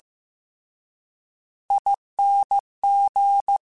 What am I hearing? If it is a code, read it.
Morse code, heard as a keyed tone, about 15 wpm, 800 Hz. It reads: ING
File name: Short Runs / Short Runs-mp3s / ESE ING @13